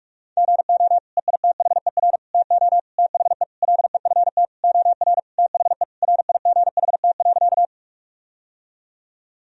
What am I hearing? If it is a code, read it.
Morse code, heard as a keyed tone, about 46 wpm, 700 Hz. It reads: GO EITHER TO THE LEFT OR THE RIGHT.
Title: Sentences from Top 1000 Words - Review Part 4 - 45wpm